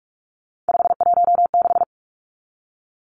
Morse code, heard as a keyed tone, about 44 wpm, 700 Hz. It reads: 516
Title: Numbers - 3-Digit 45wpm